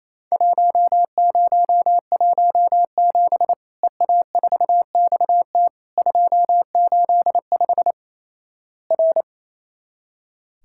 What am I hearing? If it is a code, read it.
Morse code, heard as a keyed tone, about 28 wpm, 700 Hz. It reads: 1017 EA4XT 285 R
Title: Worked All Europe DX Contest - QTC Exchange - 28wpm